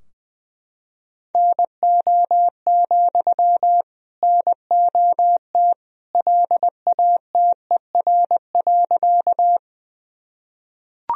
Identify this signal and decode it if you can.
Morse code, heard as a keyed tone, about 20 wpm, 700 Hz. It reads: NO, NOT LATER.
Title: Sentences from Top 1000 Words - Review Part 4 - 20wpm